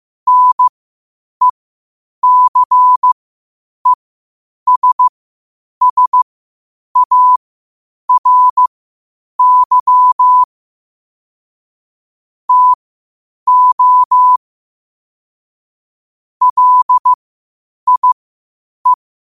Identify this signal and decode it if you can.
Morse code, heard as a keyed tone, about 15 wpm, spaced as for 8 wpm, 1000 Hz. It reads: NECESSARY TO LIE
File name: qotdfiles / 8wpm